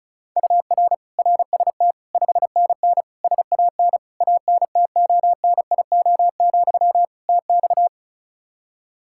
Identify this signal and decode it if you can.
Morse code, heard as a keyed tone, about 36 wpm, 700 Hz. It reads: UR RST 5NN SAN ANTONIO, TX
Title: Practice QSO - Exchange 1 - Beginning - 35wpm